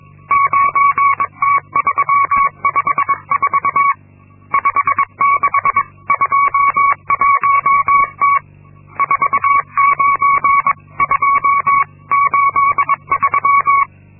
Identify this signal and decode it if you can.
Morse code, heard as a keyed tone, about 22 wpm, 1120 Hz. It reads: 9T354 5621T 49283
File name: Linksmakalnis-2